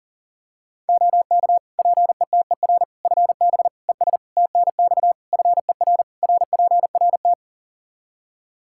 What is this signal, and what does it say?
Morse code, heard as a keyed tone, about 40 wpm, 700 Hz. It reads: OK PETER FB ES TNX FER RPRT
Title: K1USN SST QSO Practice 40wpm